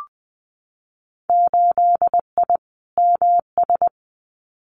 Morse code, heard as a keyed tone, about 20 wpm, 700 Hz. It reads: 8I MS
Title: Winter Field Day Exchanges Encore 20wpm